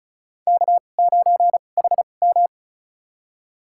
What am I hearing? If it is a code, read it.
Morse code, heard as a keyed tone, about 34 wpm, spaced as for 25 wpm, 700 Hz. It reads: K9HM
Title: USA Call Signs - 4 Characters - 2x Character Spacing - 35wpm